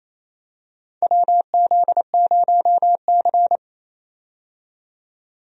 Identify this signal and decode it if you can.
Morse code, heard as a keyed tone, about 28 wpm, 700 Hz. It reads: WZ0C
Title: USA Call Signs - 4 Characters - 28wpm